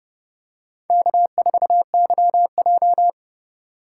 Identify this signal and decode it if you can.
Morse code, heard as a keyed tone, about 30 wpm, 700 Hz. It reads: K4YJ
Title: USA Call Signs - 4 Characters - 3x Character Spacing - 30wpm